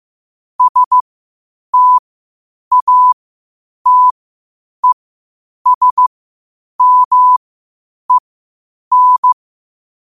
Morse code, heard as a keyed tone, about 15 wpm, spaced as for 8 wpm, 1000 Hz. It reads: STATESMEN
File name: qotdfiles / 8wpm